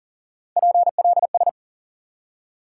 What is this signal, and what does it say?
Morse code, heard as a keyed tone, about 40 wpm, 700 Hz. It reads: PLS
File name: IQE Course 3 - Lesson 89 - YAGI - Two Elements - 40wpm